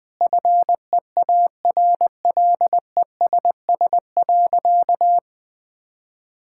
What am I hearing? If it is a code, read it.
Morse code, heard as a keyed tone, about 20 wpm, 700 Hz. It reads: FEARLESS.
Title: Sentences with 4-Word Phrases - Spoken Beginning - 20wpm